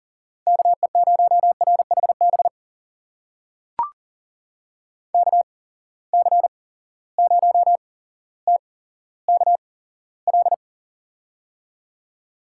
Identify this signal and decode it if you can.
Morse code, heard as a keyed tone, about 40 wpm, spaced as for 17 wpm, 700 Hz. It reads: KE0RHB KC0TKL
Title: USA Call Signs - 6 Characters - 8x Character Spacing - 40wpm